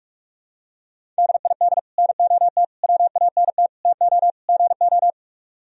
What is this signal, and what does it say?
Morse code, heard as a keyed tone, about 44 wpm, 700 Hz. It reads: DID NOT WANT TO GO